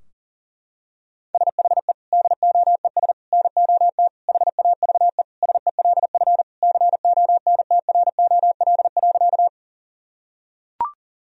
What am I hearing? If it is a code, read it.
Morse code, heard as a keyed tone, about 40 wpm, 700 Hz. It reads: SHE DOES NOT HAVE SELF CONTROL.
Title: Sentences from Top 1000 Words - Review Part 3 - 40wpm